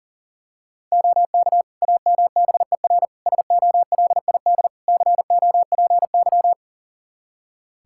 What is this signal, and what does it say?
Morse code, heard as a keyed tone, about 40 wpm, 690 Hz. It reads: OK AMBER SOLID COPY
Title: Practice QSO - Exchange 3 - More Info - 40wpm